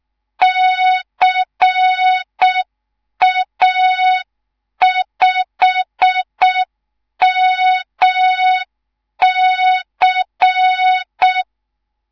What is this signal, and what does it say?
Morse code, heard as a keyed tone, about 6 wpm, 3700 Hz. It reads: CA5MC
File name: Morze-HA2MN / hang / 10GWZ HA2MN